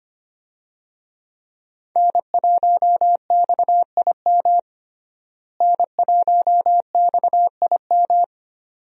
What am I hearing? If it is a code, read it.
Morse code, heard as a keyed tone, about 25 wpm, 700 Hz. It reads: N1XIM N1XIM
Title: USA Call Signs - 5 Characters - Repeated 2x - 25wpm